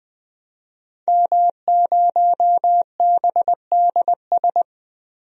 Morse code, heard as a keyed tone, about 20 wpm, 700 Hz. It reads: M0BDS
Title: DXCC-100 Call Signs - 5 Characters - 20wpm